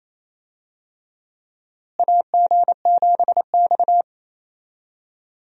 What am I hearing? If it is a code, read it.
Morse code, heard as a keyed tone, about 28 wpm, 700 Hz. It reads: AG7X
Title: USA Call Signs - 4 Characters - Repeated 3x - 28wpm